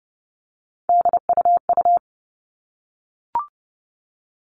Morse code, heard as a keyed tone, about 30 wpm, 700 Hz. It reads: DUU
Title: All Cut Numbers - 3-Digit 30wpm